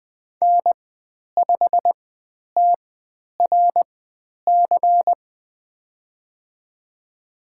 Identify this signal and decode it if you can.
Morse code, heard as a keyed tone, about 20 wpm, spaced as for 10 wpm, 700 Hz. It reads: N5TRC
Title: 102 Lesson 17g - Letter u - Call Signs 20wpm